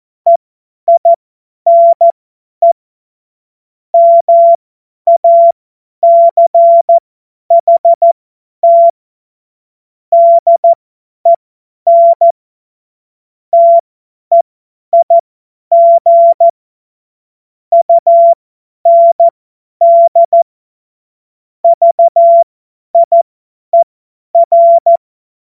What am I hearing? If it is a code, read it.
Morse code, heard as a keyed tone, about 14 wpm, spaced as for 10 wpm, 680 Hz. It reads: EINE MACHT DEN TEIG UND VIER
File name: Morsekurs / Witze / Witz 13. April 2021, 50 F70